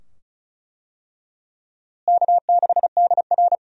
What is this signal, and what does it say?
Morse code, heard as a keyed tone, about 34 wpm, 700 Hz. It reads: K6DR
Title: USA Call Signs - 4 Characters - Repeated 2x - 35wpm